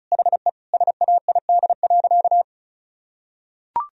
Morse code, heard as a keyed tone, about 34 wpm, 700 Hz. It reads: HE SAID.
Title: Sentences from Top 1000 Words - Review Part 2 - 35wpm